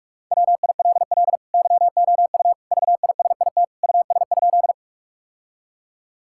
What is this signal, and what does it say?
Morse code, heard as a keyed tone, about 46 wpm, 700 Hz. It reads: WILL YOU VISIT US?
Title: Sentences from Top 1000 Words - Review Part 6 - 45wpm